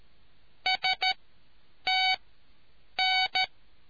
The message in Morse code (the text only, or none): STN